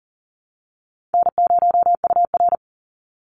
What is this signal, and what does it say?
Morse code, heard as a keyed tone, about 40 wpm, 700 Hz. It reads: N0UR
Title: Call Signs Repeated 3x - 40wpm